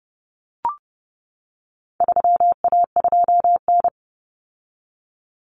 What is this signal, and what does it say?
Morse code, heard as a keyed tone, about 30 wpm, 700 Hz. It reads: 3A2N
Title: Common Cut Numbers - 4-Digit 30wpm